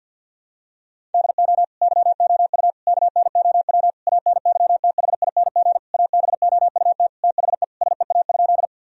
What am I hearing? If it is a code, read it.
Morse code, heard as a keyed tone, about 52 wpm, 700 Hz. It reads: DO YOU KNOW ANYTHING ABOUT THE SEA?
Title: Sentences from Top 1000 Words - Review Part 1 - 50wpm